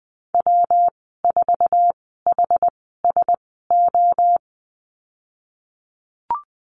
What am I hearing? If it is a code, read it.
Morse code, heard as a keyed tone, about 20 wpm, spaced as for 15 wpm, 700 Hz. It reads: W4HSO
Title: 128 Lesson 21g - Number 6 - Call Signs 20wpm